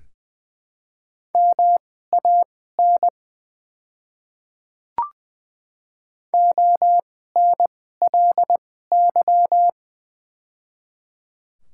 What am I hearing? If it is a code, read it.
Morse code, heard as a keyed tone, about 20 wpm, spaced as for 15 wpm, 700 Hz. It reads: MAN ONLY
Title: 2x Character Spacing - Top 300 Words - 20wpm